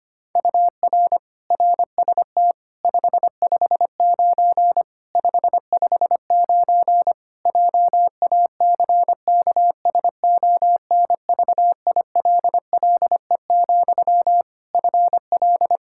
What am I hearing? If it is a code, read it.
Morse code, heard as a keyed tone, about 25 wpm, 700 Hz. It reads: UR RST 559 559 JACKSONVILLE, FL
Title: Practice QSO - Exchange 1 - Beginning - 25wpm